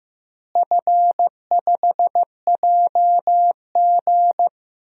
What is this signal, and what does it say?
Morse code, heard as a keyed tone, about 15 wpm, 700 Hz. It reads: F5JG
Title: DXCC-100 Call Signs - 4 Characters - Repeated 2x - 15wpm